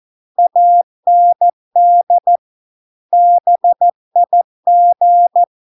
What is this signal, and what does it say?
Morse code, heard as a keyed tone, about 14 wpm, 700 Hz. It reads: AND BIG